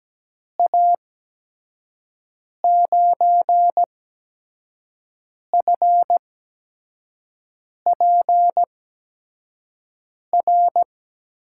Morse code, heard as a keyed tone, about 17 wpm, spaced as for 5 wpm, 700 Hz. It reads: A9FPR
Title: USA Call Signs - 6 Characters - 8x Character Spacing - 17wpm